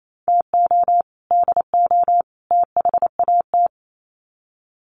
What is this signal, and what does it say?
Morse code, heard as a keyed tone, about 28 wpm, 700 Hz. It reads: TO DO THAT
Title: Sets of 3 Words 28wpm